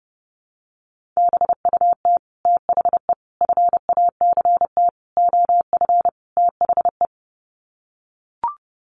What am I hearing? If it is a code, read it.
Morse code, heard as a keyed tone, about 30 wpm, 700 Hz. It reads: BUT THE FACT OF THE
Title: Sets of 5 Words Encore2 30wpm